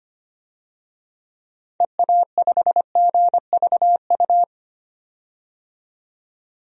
Morse code, heard as a keyed tone, about 25 wpm, 700 Hz. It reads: EA5GVU
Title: DXCC-100 Call Signs - 6 Characters - 25wpm